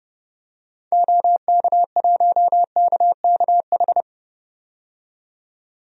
Morse code, heard as a keyed tone, about 31 wpm, 700 Hz. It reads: OK1KKH
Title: DXCC-100 Call Signs - 6 Characters - 30wpm